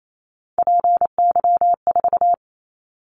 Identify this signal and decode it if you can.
Morse code, heard as a keyed tone, about 28 wpm, 700 Hz. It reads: PY4
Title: Callsign Prefixes for Top 50 DXCC - 28wpm